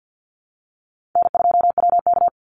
Morse code, heard as a keyed tone, about 49 wpm, 700 Hz. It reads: N2WK
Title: Call Signs - 50wpm